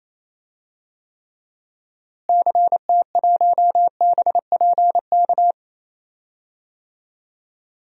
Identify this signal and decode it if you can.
Morse code, heard as a keyed tone, about 28 wpm, 700 Hz. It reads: CT1BPK
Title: DXCC-100 Call Signs - 6 Characters - Repeated 3x - 28wpm